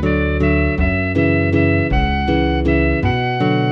There is no code.